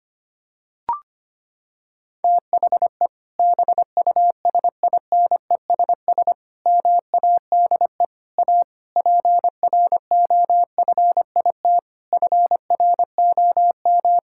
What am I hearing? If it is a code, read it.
Morse code, heard as a keyed tone, about 25 wpm, 700 Hz. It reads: THE BUSINESS MADE A PROFIT FROM
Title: Sentences from Top 1000 Words - Review Part 5 - 25wpm